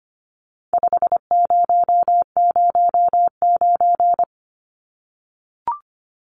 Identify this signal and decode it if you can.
Morse code, heard as a keyed tone, about 25 wpm, 700 Hz. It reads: 5009
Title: Numbers - 4-Digit 25wpm